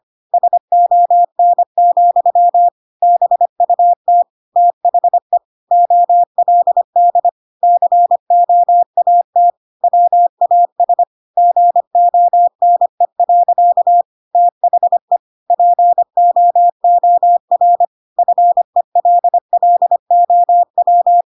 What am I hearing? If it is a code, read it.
Morse code, heard as a keyed tone, about 25 wpm, 700 Hz. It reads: SON, BUT THE OLD COAT WAS GONE. THE POOR FELLOW
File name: Pinocchio / Pinocchio ch8 25 25 ik1hge